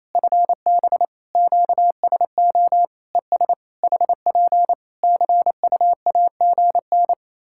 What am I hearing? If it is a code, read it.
Morse code, heard as a keyed tone, about 27 wpm, 700 Hz. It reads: FB QSO ES HP CUAGN
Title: Practice QSO - Exchange 4 - Ending - 28wpm